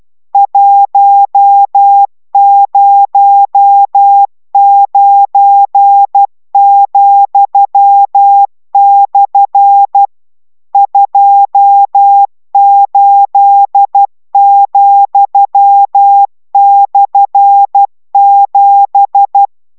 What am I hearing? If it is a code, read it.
Morse code, heard as a keyed tone, about 12 wpm, 800 Hz. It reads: 109,/ 28,/7